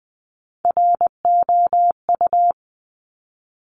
Morse code, heard as a keyed tone, about 20 wpm, 700 Hz. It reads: ROU
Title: Top Three Letter Combinations in Words 20wpm